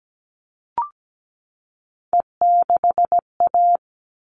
Morse code, heard as a keyed tone, about 17 wpm, 700 Hz. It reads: E6A
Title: All Cut Numbers - 3-Digit 17wpm